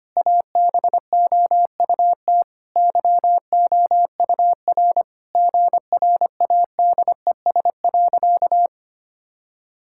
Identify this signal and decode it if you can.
Morse code, heard as a keyed tone, about 25 wpm, 700 Hz. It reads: ABOUT YOUR GRADES.
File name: practice / 25wpm / Sentences with 4-Word Phrases - 25wpm